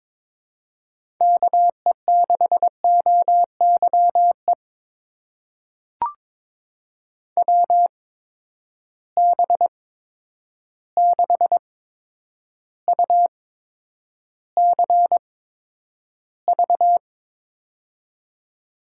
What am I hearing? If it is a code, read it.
Morse code, heard as a keyed tone, about 22 wpm, spaced as for 9 wpm, 700 Hz. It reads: KE6OYE WB6UCV
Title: USA Call Signs - 6 Characters - 8x Character Spacing - 22wpm